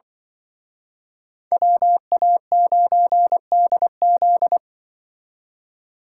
Morse code, heard as a keyed tone, about 24 wpm, 700 Hz. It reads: WA9DZ
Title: Speed Racing - USA 2x2 Call Signs 20wpm